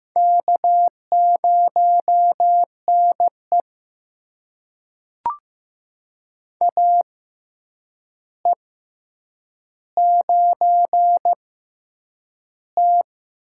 Morse code, heard as a keyed tone, about 15 wpm, spaced as for 8 wpm, 700 Hz. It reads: K0NE AE9T
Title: USA Call Signs - 6x Character Spacing - 15wpm